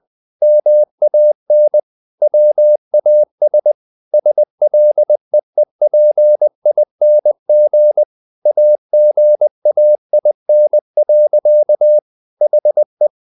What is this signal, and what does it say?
Morse code, heard as a keyed tone, about 20 wpm, 600 Hz. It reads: MAN WAS SLEEPING AGAIN. HE
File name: the old man and the sea part 2 20wpm